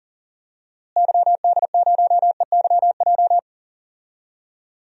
Morse code, heard as a keyed tone, about 39 wpm, 700 Hz. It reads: YD0EYJ
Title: DXCC-100 Call Signs - 6 Characters - Repeated 2x - 40wpm